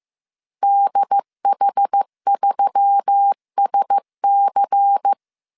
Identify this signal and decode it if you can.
Morse code, heard as a keyed tone, about 15 wpm, 800 Hz. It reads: DH3SC